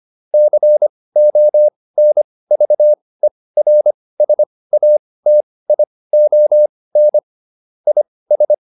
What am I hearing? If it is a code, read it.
Morse code, heard as a keyed tone, about 25 wpm, spaced as for 18 wpm, 600 Hz. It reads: CONVERSATION IS